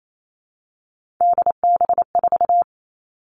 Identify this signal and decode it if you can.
Morse code, heard as a keyed tone, about 28 wpm, 700 Hz. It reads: DB4
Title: All Cut Numbers - 3-Digit 28wpm